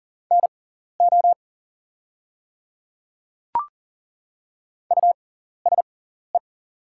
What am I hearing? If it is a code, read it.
Morse code, heard as a keyed tone, about 37 wpm, spaced as for 14 wpm, 690 Hz. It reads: NO USE
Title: 6x Character Spacing - Top 100 Words - 40wpm